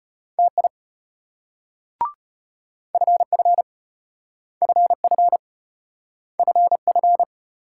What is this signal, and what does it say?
Morse code, heard as a keyed tone, about 35 wpm, 700 Hz. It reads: TI FF FF FF